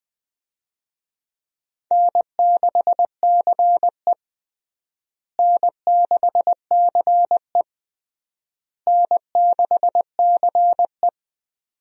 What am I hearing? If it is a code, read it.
Morse code, heard as a keyed tone, about 20 wpm, 700 Hz. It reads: N6CE N6CE N6CE
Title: USA Call Signs - 4 Characters - Repeated 3x - 20wpm